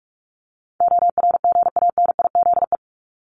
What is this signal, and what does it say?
Morse code, heard as a keyed tone, about 44 wpm, 700 Hz. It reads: ORGANIZE
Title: Top 1751-2000 Words 45wpm